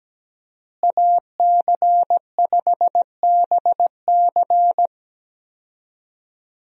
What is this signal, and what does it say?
Morse code, heard as a keyed tone, about 17 wpm, 700 Hz. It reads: AC5BC
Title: USA Call Signs - 5 Characters - Repeated 2x - 17wpm